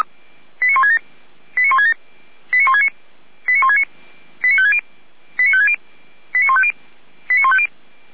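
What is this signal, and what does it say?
Five-tone selective call (ZVEI), heard as quick runs of five tones, tones of about 70 ms. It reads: 89167 89167 89169 89169 89560 89560 89150 89150
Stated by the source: ZVEI five-tone selective call received with an SDR, logged by the Feuerwehr Orlamünde fire service.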